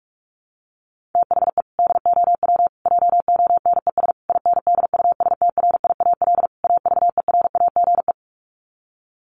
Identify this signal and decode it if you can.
Morse code, heard as a keyed tone, about 44 wpm, 700 Hz. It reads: THE DOW JONES INDUSTRIAL AVERAGE
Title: Sets of 5 Words 45wpm